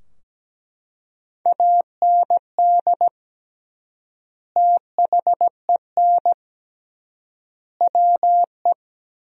Morse code, heard as a keyed tone, about 17 wpm, 700 Hz. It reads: AND THEN WE
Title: 3x Word-Spacing - Sentences from Top 300 Words - 17wpm